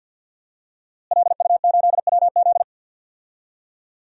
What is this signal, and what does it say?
Morse code, heard as a keyed tone, about 50 wpm, 700 Hz. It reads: LU8WZ
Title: DXCC-100 Call Signs - 5 Characters - Repeated 3x - 50wpm